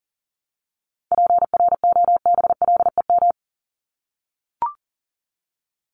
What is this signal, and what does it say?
Morse code, heard as a keyed tone, about 40 wpm, 700 Hz. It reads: PROBLEM
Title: Top 500 Words Repeated 40wpm